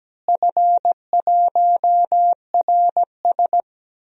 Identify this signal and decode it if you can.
Morse code, heard as a keyed tone, about 17 wpm, 700 Hz. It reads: F1RS
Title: DXCC-100 Call Signs - 4 Characters - Repeated 2x - 17wpm